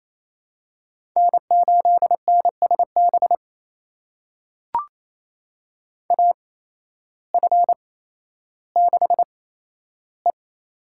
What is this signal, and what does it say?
Morse code, heard as a keyed tone, about 28 wpm, spaced as for 13 wpm, 700 Hz. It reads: N8NSB AF6E
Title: USA Call Signs - 5 Characters - 8x Character Spacing - 28wpm